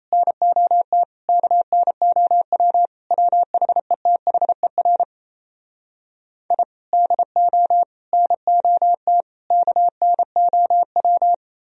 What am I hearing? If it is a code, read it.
Morse code, heard as a keyed tone, about 38 wpm, 700 Hz. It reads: NOT KNOW WHETHER I DO NOT KNOW